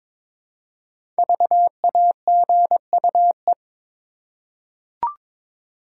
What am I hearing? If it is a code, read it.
Morse code, heard as a keyed tone, about 22 wpm, 700 Hz. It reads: VAGUE
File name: practice / 22wpm / 5 Letter Words - First 3 Letters Given - 22wpm